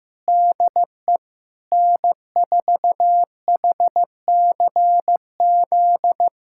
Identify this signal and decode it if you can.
Morse code, heard as a keyed tone, about 16 wpm, 700 Hz. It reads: DE N4HCZ
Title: USA Callsign DE Callsign 15wpm